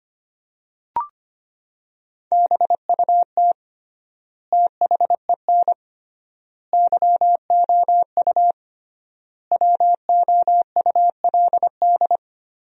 Morse code, heard as a keyed tone, about 25 wpm, 700 Hz. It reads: BUT THEN YOU WOULD
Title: 3x Word-Spacing - Sentences from Top 300 Words - 25wpm